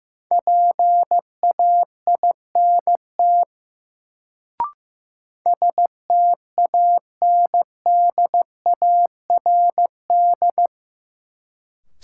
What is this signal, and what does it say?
Morse code, heard as a keyed tone, about 15 wpm, 700 Hz. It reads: PAINT STANDARD
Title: Sentences from Top 1000 Words - Review Part 6 - New Word Drill - 15wpm